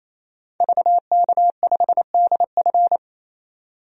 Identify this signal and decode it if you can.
Morse code, heard as a keyed tone, about 28 wpm, 700 Hz. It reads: VK5DF
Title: DXCC-100 Call Signs - 5 Characters - 28wpm